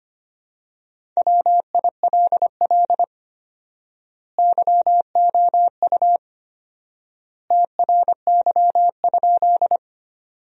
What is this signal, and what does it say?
Morse code, heard as a keyed tone, about 25 wpm, 700 Hz. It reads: WILL YOU TRY?
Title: 4x Word-Spacing - Sentences from Top 200 Words - 25wpm